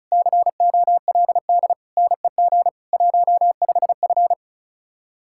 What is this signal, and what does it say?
Morse code, heard as a keyed tone, about 44 wpm, 700 Hz. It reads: COLD NEG 15F